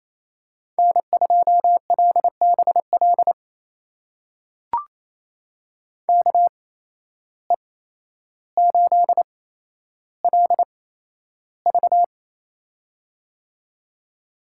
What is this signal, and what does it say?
Morse code, heard as a keyed tone, about 28 wpm, spaced as for 12 wpm, 700 Hz. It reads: N2LBL KE8LV
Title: USA Call Signs - 5 Characters - 8x Character Spacing - 28wpm